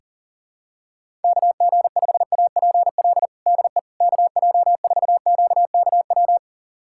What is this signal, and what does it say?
Morse code, heard as a keyed tone, about 40 wpm, 700 Hz. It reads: KG5APL DE KJ4QKW
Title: USA Callsign DE Callsign 40wpm